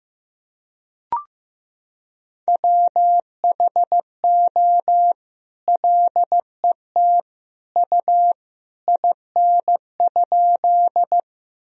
Morse code, heard as a keyed tone, about 16 wpm, 700 Hz. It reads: WHO LET U IN?
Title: Sentences from Top 300 Words and Abbreviations 15wpm